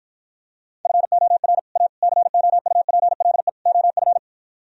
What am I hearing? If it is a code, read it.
Morse code, heard as a keyed tone, about 53 wpm, 700 Hz. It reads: FOR A COUPLE OF